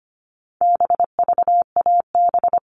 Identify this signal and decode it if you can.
Morse code, heard as a keyed tone, about 25 wpm, 700 Hz. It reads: BVAB
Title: All Cut Numbers - 4-Digit 25wpm